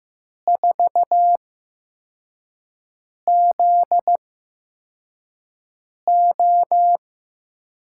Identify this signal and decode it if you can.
Morse code, heard as a keyed tone, about 15 wpm, spaced as for 4 wpm, 700 Hz. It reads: 4ZO
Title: USA Call Signs - 6 Characters - 8x Character Spacing - 15wpm